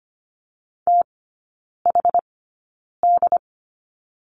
Morse code, heard as a keyed tone, about 25 wpm, spaced as for 9 wpm, 690 Hz. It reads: THD
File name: Lesson 14e - Number 2 - Three Characters 25wpm